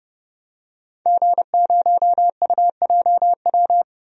Morse code, heard as a keyed tone, about 30 wpm, 700 Hz. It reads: G0UJW
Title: DXCC-100 Call Signs - 5 Characters - 30wpm